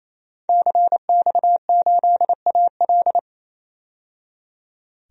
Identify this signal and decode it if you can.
Morse code, heard as a keyed tone, about 28 wpm, 700 Hz. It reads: CX8AL
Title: DXCC-100 Call Signs - 5 Characters - 28wpm